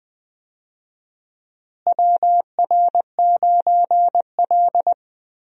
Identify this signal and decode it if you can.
Morse code, heard as a keyed tone, about 20 wpm, 700 Hz. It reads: WR9L
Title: USA Call Signs - 4 Characters - Repeated 3x - 20wpm